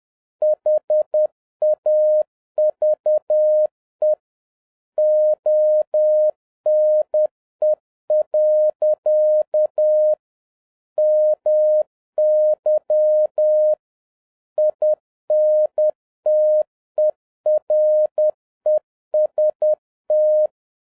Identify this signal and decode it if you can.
Morse code, heard as a keyed tone, about 10 wpm, 610 Hz. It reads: HAVE ONE. MY INTEREST